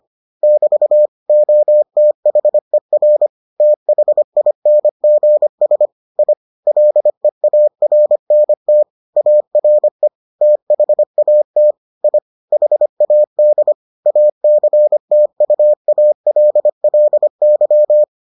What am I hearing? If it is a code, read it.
Morse code, heard as a keyed tone, about 25 wpm, 600 Hz. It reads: <BT> OTHER THINGS I LEARNT ARE THAT I HAD ACTUALLY